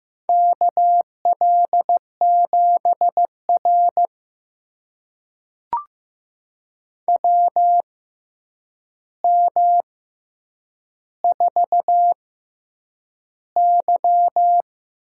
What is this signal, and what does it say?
Morse code, heard as a keyed tone, about 15 wpm, spaced as for 8 wpm, 700 Hz. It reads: KL7R WM4Y
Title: USA Call Signs - 4 Characters - 6x Character Spacing - 15wpm